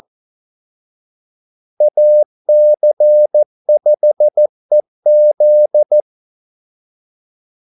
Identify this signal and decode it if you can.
Morse code, heard as a keyed tone, about 14 wpm, 600 Hz. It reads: AC5EZ
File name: Callsigns-14wpm